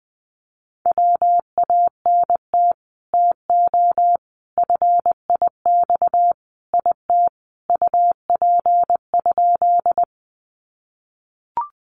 Morse code, heard as a keyed tone, about 20 wpm, 700 Hz. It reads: WANT TO FIX IT UP?
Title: Sentences from Top 1400 Words 20wpm